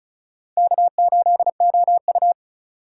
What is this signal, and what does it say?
Morse code, heard as a keyed tone, about 34 wpm, 700 Hz. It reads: K8OU
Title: USA Call Signs - 4 Characters - 2x Character Spacing - 35wpm